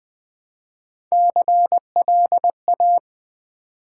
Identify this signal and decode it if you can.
Morse code, heard as a keyed tone, about 20 wpm, 700 Hz. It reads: CLA
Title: Build A Word - 1 Letter At A Time - 5-Letter Words - 20wpm